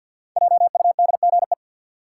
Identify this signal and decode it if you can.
Morse code, heard as a keyed tone, about 48 wpm, 700 Hz. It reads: JUDGE
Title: Five Letter Words - 50wpm